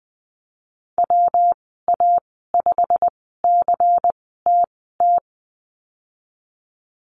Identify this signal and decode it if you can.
Morse code, heard as a keyed tone, about 20 wpm, spaced as for 15 wpm, 700 Hz. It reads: WA5CTT
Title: 147 Lesson 24g - Letter y - Call Signs 20wpm